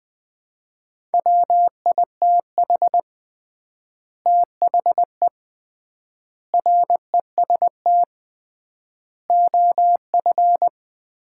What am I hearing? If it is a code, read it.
Morse code, heard as a keyed tone, about 20 wpm, 700 Hz. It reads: WITH THE REST OF
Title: 3x Word-Spacing - Sets of 5 Words - 20wpm